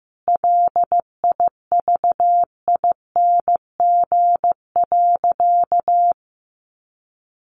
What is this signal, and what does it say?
Morse code, heard as a keyed tone, about 15 wpm, 700 Hz. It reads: LIVING.
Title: Sentences from Top 1400 Words 15wpm